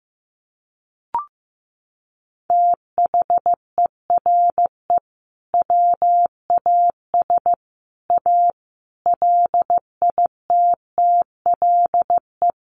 Morse code, heard as a keyed tone, about 15 wpm, 700 Hz. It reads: THERE WAS A LITTLE
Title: Sentences from Top 1500 Words 15wpm